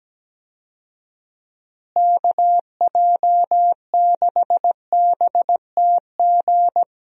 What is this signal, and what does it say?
Morse code, heard as a keyed tone, about 17 wpm, 700 Hz. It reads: KJ6BTG
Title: USA Call Signs - 6 Characters - Repeated 3x - 17wpm